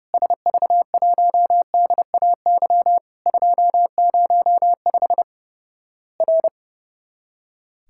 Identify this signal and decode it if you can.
Morse code, heard as a keyed tone, about 32 wpm, 700 Hz. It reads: SV1DAY 205 R
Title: Worked All Europe DX Contest - QTC Exchange - 30wpm